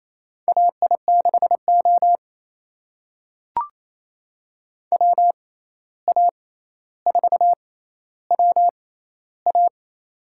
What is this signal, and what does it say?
Morse code, heard as a keyed tone, about 28 wpm, spaced as for 13 wpm, 700 Hz. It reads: AI6O WA4WA
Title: USA Call Signs - 6x Character Spacing - 28wpm